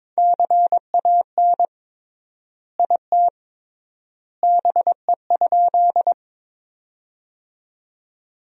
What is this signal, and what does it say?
Morse code, heard as a keyed tone, about 22 wpm, 700 Hz. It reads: CAN IT BE?
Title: 3x Word-Spacing - Sentences from Top 100 Words - 22wpm